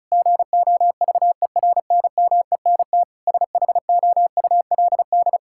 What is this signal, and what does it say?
Morse code, heard as a keyed tone, about 36 wpm, 700 Hz. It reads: GOVERNMENT SHOULD
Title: Sentences from Top 1600 Words 35wpm